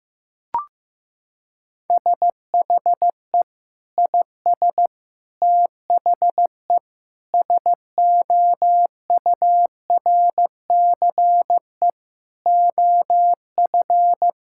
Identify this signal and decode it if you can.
Morse code, heard as a keyed tone, about 15 wpm, 700 Hz. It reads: SHE IS THE SOURCE OF
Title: Sentences from Top 1000 Words - Review Part 5 - 15wpm